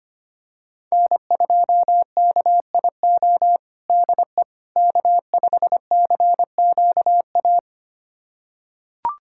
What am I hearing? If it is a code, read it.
Morse code, heard as a keyed tone, about 25 wpm, 700 Hz. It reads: N2KIO DE K5CQA
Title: USA Callsign DE Callsign - 5 Characters - 25wpm